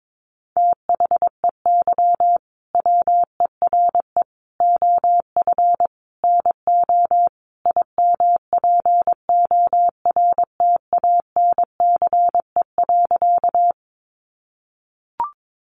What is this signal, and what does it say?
Morse code, heard as a keyed tone, about 22 wpm, 700 Hz. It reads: THEY WERE OF NO IMPORTANCE.
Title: Sentences from Top 1200 Words 22wpm